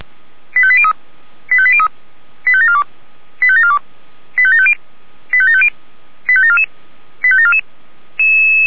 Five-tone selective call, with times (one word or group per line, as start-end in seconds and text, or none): 0.6-0.9 85803
1.5-1.9 85803
2.5-2.8 85742
3.4-3.8 85742
4.4-4.7 85750
5.3-5.7 85750
6.3-6.6 85744
7.2-7.6 85744